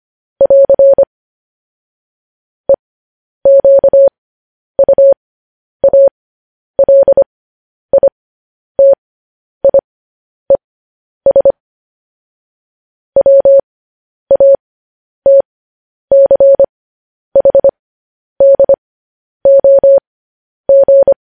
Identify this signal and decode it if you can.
Morse code, heard as a keyed tone, about 25 wpm, spaced as for 10 wpm, 550 Hz. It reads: <AR> EQUALITIES WATCHDOG
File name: podcast / News headlines-1x-25x10